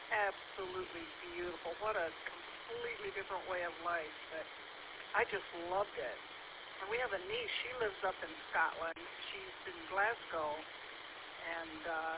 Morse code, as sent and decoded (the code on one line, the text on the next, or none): none